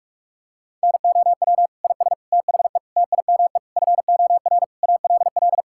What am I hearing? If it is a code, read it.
Morse code, heard as a keyed tone, about 46 wpm, 700 Hz. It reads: NOW IS THE TIME FOR ALL
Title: Pangram Challenge v2 - 45wpm